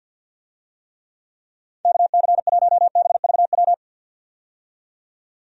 Morse code, heard as a keyed tone, about 51 wpm, 700 Hz. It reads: KC1BVW
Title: USA Call Signs - 6 Characters - 50wpm